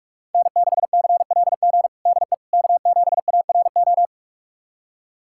Morse code, heard as a keyed tone, about 46 wpm, 700 Hz. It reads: N6CLG DE K7ARO
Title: USA Callsign DE Callsign - 5 Characters - 45wpm